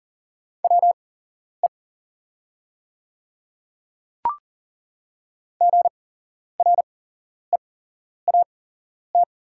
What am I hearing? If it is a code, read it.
Morse code, heard as a keyed tone, about 38 wpm, spaced as for 11 wpm, 700 Hz. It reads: WE GREAT
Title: 8x Character Spacing - Top 200 Words - 40wpm